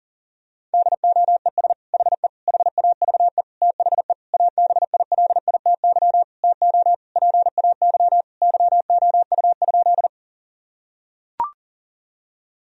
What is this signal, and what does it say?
Morse code, heard as a keyed tone, about 40 wpm, 700 Hz. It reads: DOES HE HAVE THE ABILITY TO PAY YOU?
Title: Sentences from Top 1000 Words - Review Part 3 - 40wpm